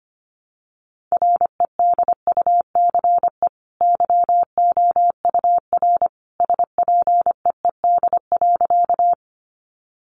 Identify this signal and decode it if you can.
Morse code, heard as a keyed tone, about 25 wpm, 700 Hz. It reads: REDUCE YOUR SPEED.
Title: Sentences from Top 1100 Words 25wpm